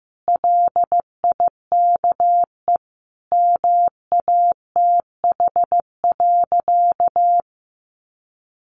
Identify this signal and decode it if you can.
Morse code, heard as a keyed tone, about 15 wpm, 700 Hz. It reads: LIKE MATH.
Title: Easy Phrases 15wpm